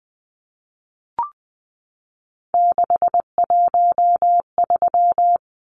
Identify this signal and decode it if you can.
Morse code, heard as a keyed tone, about 20 wpm, 700 Hz. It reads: 613
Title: Numbers - 3-Digit 20wpm